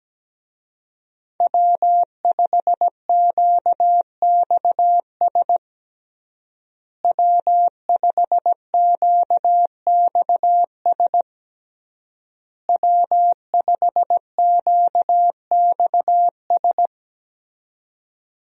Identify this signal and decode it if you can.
Morse code, heard as a keyed tone, about 17 wpm, 700 Hz. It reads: W5QXS W5QXS W5QXS
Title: USA Call Signs - 5 Characters - Repeated 3x - 17wpm